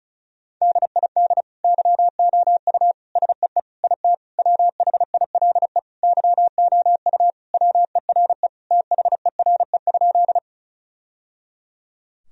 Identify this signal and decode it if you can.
Morse code, heard as a keyed tone, about 36 wpm, 700 Hz. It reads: DID YOU SEE IT WHILE YOU WERE THERE?
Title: Sentences from Top 1000 Words - Review Part 1 - 35wpm